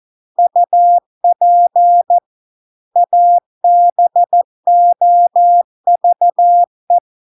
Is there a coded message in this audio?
Yes: Morse code from 0.4 to 7.0 s, heard as a keyed tone, about 14 wpm, 700 Hz. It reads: UP ABOVE